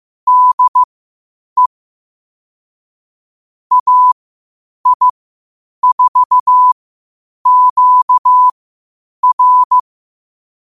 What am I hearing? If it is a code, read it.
Morse code, heard as a keyed tone, about 15 wpm, spaced as for 8 wpm, 1000 Hz. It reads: DE AI4QR
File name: qotdfiles / 8wpm